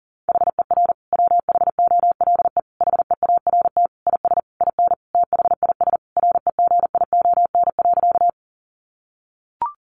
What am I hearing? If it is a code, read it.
Morse code, heard as a keyed tone, about 40 wpm, 700 Hz. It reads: HER WHOLE HEART IS IN THIS REGION.
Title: Sentences from Top 1500 Words 40wpm